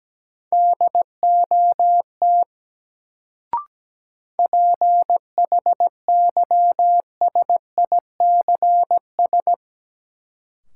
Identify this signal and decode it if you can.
Morse code, heard as a keyed tone, about 17 wpm, 700 Hz. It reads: DOT PHYSICS
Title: Sentences from Top 1500 Words - New Word Drill Alt - 17wpm